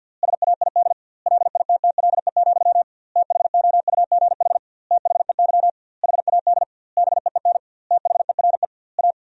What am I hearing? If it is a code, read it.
Morse code, heard as a keyed tone, about 50 wpm, 700 Hz. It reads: SAID LITTLE, THOUGH THEY HAD BEEN THERE A
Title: 2x Word-Spacing - Sentences from Top 400 Words - 50wpm